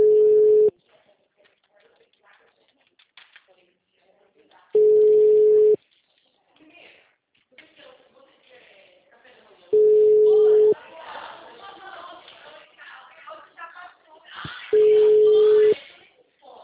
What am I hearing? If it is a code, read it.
Ringback tone (Europe, 425 Hz), with speech.